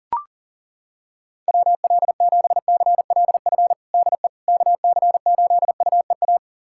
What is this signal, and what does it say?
Morse code, heard as a keyed tone, about 40 wpm, 700 Hz. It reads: WL7CLF DE KC8UEA
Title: USA Callsign DE Callsign - 6 Characters - 40wpm